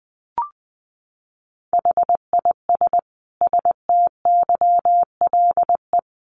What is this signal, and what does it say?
Morse code, heard as a keyed tone, about 20 wpm, 700 Hz. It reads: HIS STYLE
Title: Sentences from Top 800 Words 20wpm